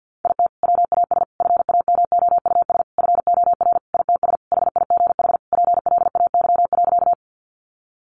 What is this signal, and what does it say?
Morse code, heard as a keyed tone, about 50 wpm, 700 Hz. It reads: IT WAS FAMOUS FOR ITS HIGH PLAY.
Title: Sentences from Top 1500 Words 50wpm